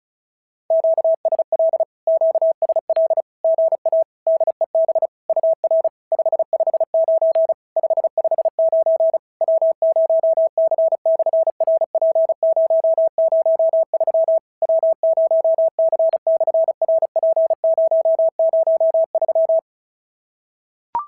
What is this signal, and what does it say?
Morse code, heard as a keyed tone, about 33 wpm, 650 Hz. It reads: QSL QSL GA DEB UR 559 559 W0C/RP003 W0C/RP003
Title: SOTA QSO Practice 35wpm